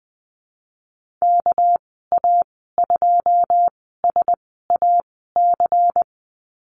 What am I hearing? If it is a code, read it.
Morse code, heard as a keyed tone, about 19 wpm, spaced as for 14 wpm, 700 Hz. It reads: KA2SAC
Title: 200 Lesson 32g - Letter k - Call Signs 20wpm